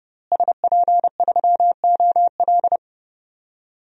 Morse code, heard as a keyed tone, about 30 wpm, 700 Hz. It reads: SP3OL
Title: DXCC-100 Call Signs - 5 Characters - Repeated 2x - 30wpm